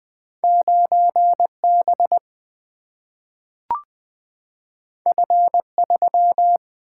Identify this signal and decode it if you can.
Morse code, heard as a keyed tone, about 20 wpm, 700 Hz. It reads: 9 B F 3
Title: Single-Letter Single-Number 20wpm